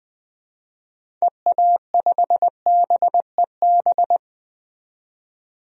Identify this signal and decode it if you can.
Morse code, heard as a keyed tone, about 20 wpm, 700 Hz. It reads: EA5BEB
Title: DXCC-100 Call Signs - 6 Characters - 20wpm